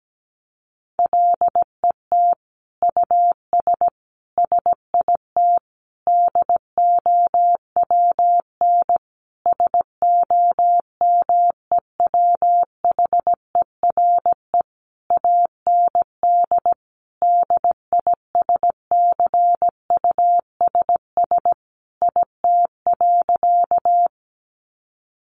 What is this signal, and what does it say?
Morse code, heard as a keyed tone, about 17 wpm, 700 Hz. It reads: LET US SIT DOWN SOMEWHERE AND DISCUSS IT.
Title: Sentences from Top 1400 Words 17wpm